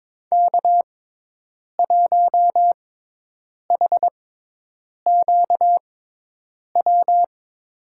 Morse code, heard as a keyed tone, about 22 wpm, spaced as for 8 wpm, 700 Hz. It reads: K1HQW
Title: USA Call Signs - 5 Characters - 6x Character Spacing - 22wpm